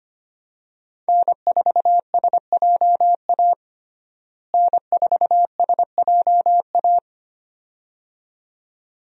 Morse code, heard as a keyed tone, about 25 wpm, 700 Hz. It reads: N4SJA N4SJA
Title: USA Call Signs - 5 Characters - Repeated 2x - 25wpm